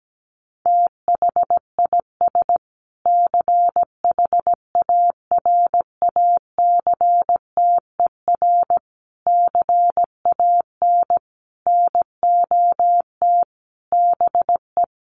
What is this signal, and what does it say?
Morse code, heard as a keyed tone, about 17 wpm, 700 Hz. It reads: THIS CHARACTER CAN NOT BE
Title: Sets of 5 Words Encore2 17wpm